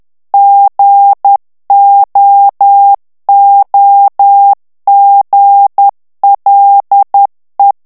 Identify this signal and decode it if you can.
Morse code, heard as a keyed tone, about 10 wpm, 800 Hz. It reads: GOOGLE
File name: morse mp3 / ITALIEN texte clair 1 vitesse 10 mots minute